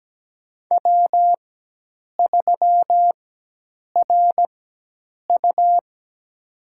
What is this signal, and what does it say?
Morse code, heard as a keyed tone, about 17 wpm, spaced as for 8 wpm, 700 Hz. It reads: W3RU
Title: USA Call Signs - 4x Character Spacing - 17wpm